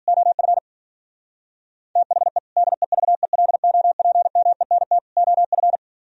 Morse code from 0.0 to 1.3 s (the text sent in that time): OF